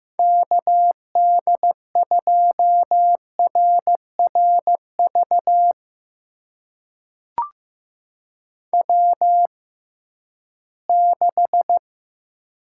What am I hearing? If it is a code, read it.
Morse code, heard as a keyed tone, about 15 wpm, spaced as for 11 wpm, 700 Hz. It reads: KD2RRV W6
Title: USA Call Signs - 6x Character Spacing - 15wpm